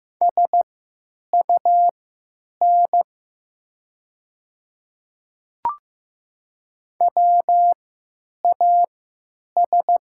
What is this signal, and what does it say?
Morse code, heard as a keyed tone, about 15 wpm, spaced as for 9 wpm, 690 Hz. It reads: SUN WAS
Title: 3x Character Spacing - Top 300 Words - 15wpm